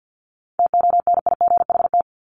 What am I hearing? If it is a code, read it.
Morse code, heard as a keyed tone, about 50 wpm, 700 Hz. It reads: TONIGHT
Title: Top 501-750 Words 50wpm